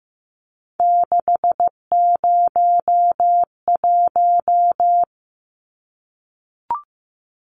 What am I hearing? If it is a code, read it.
Morse code, heard as a keyed tone, about 15 wpm, 700 Hz. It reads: 601
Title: Numbers - 3-Digit 15wpm